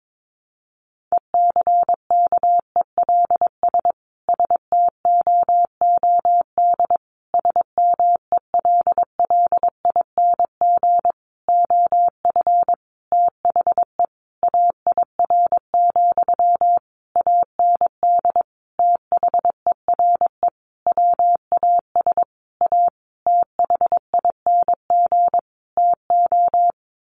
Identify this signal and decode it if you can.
Morse code, heard as a keyed tone, about 22 wpm, 700 Hz. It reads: ECKELS STOOD SMELLING OF THE AIR, AND THERE WAS A THING TO
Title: A Sound Of Thunder 22wpm